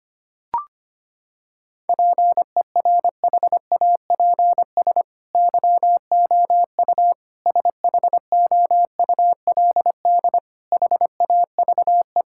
Morse code, heard as a keyed tone, about 25 wpm, 700 Hz. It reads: PERHAPS YOU SHOULD HAVE
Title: Sentences from Top 1000 Words - Review Part 6 - 25wpm